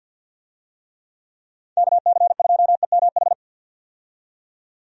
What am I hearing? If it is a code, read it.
Morse code, heard as a keyed tone, about 49 wpm, 700 Hz. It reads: KC2EMH
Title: USA Call Signs - 6 Characters - 50wpm